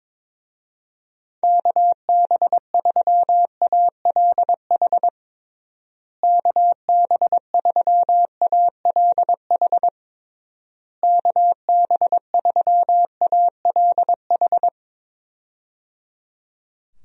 Morse code, heard as a keyed tone, about 22 wpm, 700 Hz. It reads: KB3ALH KB3ALH KB3ALH
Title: USA Call Signs - 6 Characters - Repeated 3x - 22wpm